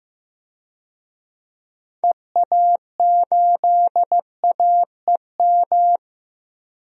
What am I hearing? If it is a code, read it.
Morse code, heard as a keyed tone, about 15 wpm, 700 Hz. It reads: EA8AEM
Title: DXCC-100 Call Signs - 6 Characters - Repeated 3x - 15wpm